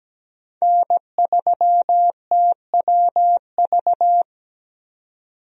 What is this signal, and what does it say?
Morse code, heard as a keyed tone, about 17 wpm, 700 Hz. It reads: N3TWV
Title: USA Call Signs - 5 Characters - 17wpm